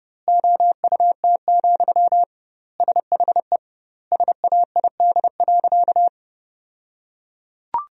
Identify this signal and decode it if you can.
Morse code, heard as a keyed tone, about 32 wpm, 700 Hz. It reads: OUT, SHE SAID.
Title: 2x Word-Spacing - Sentences from Top 500 Words - 30wpm